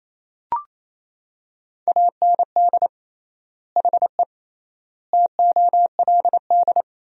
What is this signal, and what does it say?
Morse code, heard as a keyed tone, about 27 wpm, 700 Hz. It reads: AND HE TOLD
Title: 3x Word-Spacing - Sentences from Top 500 Words - 28wpm